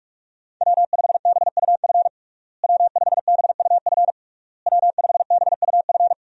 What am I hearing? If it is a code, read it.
Morse code, heard as a keyed tone, about 46 wpm, 700 Hz. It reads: W5BUF W5BUF W5BUF
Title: USA Call Signs - 5 Characters - Repeated 3x - 45wpm